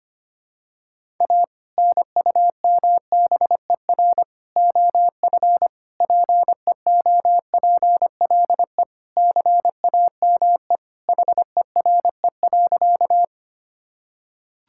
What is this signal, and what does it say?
Morse code, heard as a keyed tone, about 25 wpm, 700 Hz. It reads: A NUMBER OF PEOPLE CAME HERE.
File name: practice / 25wpm / Sentences with 2-Word Phrases - 25wpm